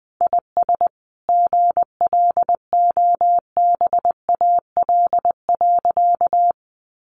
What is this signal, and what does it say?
Morse code, heard as a keyed tone, about 20 wpm, 700 Hz. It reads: IS GLOBAL.
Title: Sentences from Top 1200 Words 20wpm